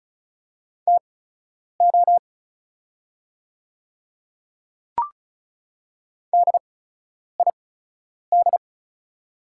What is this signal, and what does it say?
Morse code, heard as a keyed tone, about 36 wpm, spaced as for 10 wpm, 700 Hz. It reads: TO DID
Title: 8x Character Spacing - Top 100 Words - 35wpm